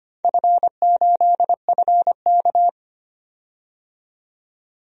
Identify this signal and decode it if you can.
Morse code, heard as a keyed tone, about 25 wpm, 700 Hz. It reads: F8FK
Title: DXCC-100 Call Signs - 4 Characters - 25wpm